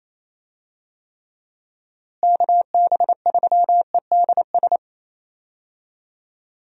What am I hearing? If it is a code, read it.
Morse code, heard as a keyed tone, about 27 wpm, 700 Hz. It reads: KB3EDS
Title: USA Call Signs - 6 Characters - Repeated 3x - 28wpm